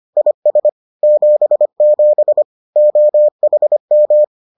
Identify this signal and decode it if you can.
Morse code, heard as a keyed tone, about 25 wpm, 600 Hz. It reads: IS 77 OHM